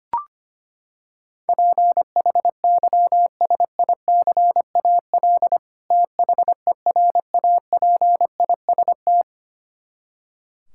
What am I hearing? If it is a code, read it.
Morse code, heard as a keyed tone, about 25 wpm, 700 Hz. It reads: PHYSICAL THERAPIST
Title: Practice QSO - Exchange 3 - More Info - 25wpm